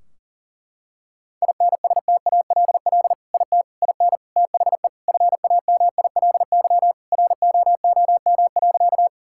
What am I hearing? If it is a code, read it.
Morse code, heard as a keyed tone, about 40 wpm, 700 Hz. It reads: INSTALL IT IN THE FAMILY ROOM.
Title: Sentences from Top 1600 Words 40wpm